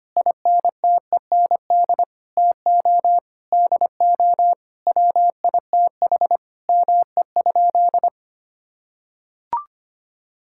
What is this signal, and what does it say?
Morse code, heard as a keyed tone, about 25 wpm, 700 Hz. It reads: INTEND TO DO WITH ME?
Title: Sentences from Top 1600 Words 25wpm